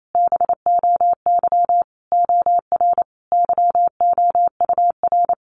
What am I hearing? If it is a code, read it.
Morse code, heard as a keyed tone, about 28 wpm, 700 Hz. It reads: BOY OR YOUR
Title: Sentences from Top 1500 Words 28wpm